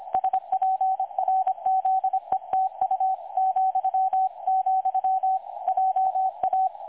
Morse code, heard as a keyed tone, about 25 wpm, 760 Hz. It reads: SPRZETU ,, JA
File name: dlazaawansowanych / nagraniazpasma / websdr recording 3565.0kHz